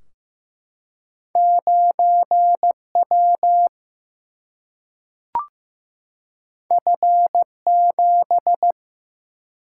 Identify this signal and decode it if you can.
Morse code, heard as a keyed tone, about 15 wpm, 700 Hz. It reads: 9 W F 7
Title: Single-Letter Single-Number 15wpm